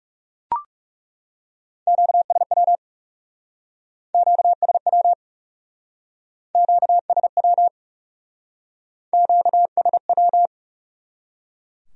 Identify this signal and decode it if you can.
Morse code, heard as a keyed tone, about 38 wpm, 700 Hz. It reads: QSW QSW QSW QSW